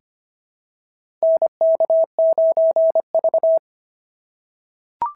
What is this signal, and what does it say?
Morse code, heard as a keyed tone, about 23 wpm, 650 Hz. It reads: NK9V
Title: SOTA QSO Practice 25wpm